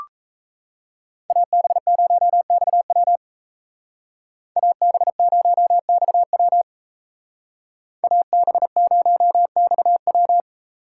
Morse code, heard as a keyed tone, about 35 wpm, 700 Hz. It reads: AB0XW AB0XW AB0XW
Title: Speed Racing - USA 2x2 Call Signs 28wpm